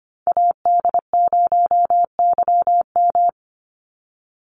Morse code, heard as a keyed tone, about 25 wpm, 700 Hz. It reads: AD0YM
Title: NA SOTA Chaser Callsigns 25wpm